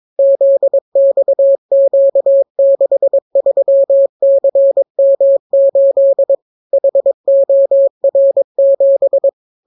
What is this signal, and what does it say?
Morse code, heard as a keyed tone, about 22 wpm, 550 Hz. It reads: ZXQ63CM8 HOR7